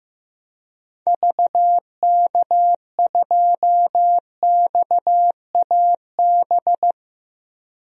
Morse code, heard as a keyed tone, about 15 wpm, 700 Hz. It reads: VK2XAB
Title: DXCC-100 Call Signs - 6 Characters - Repeated 2x - 15wpm